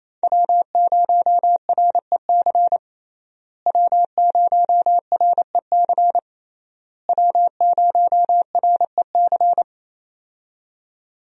Morse code, heard as a keyed tone, about 27 wpm, 700 Hz. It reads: W0REC W0REC W0REC
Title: USA Call Signs - 5 Characters - Repeated 3x - 28wpm